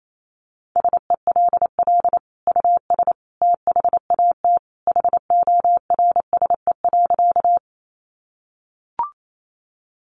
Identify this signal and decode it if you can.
Morse code, heard as a keyed tone, about 28 wpm, 700 Hz. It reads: SELL US THAT HORSE.
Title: Sentences from Top 1100 Words 28wpm